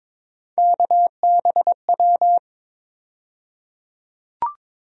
Morse code, heard as a keyed tone, about 22 wpm, 700 Hz. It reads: KBW
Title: USA Call Sign Postfixes - 3 Letters 22wpm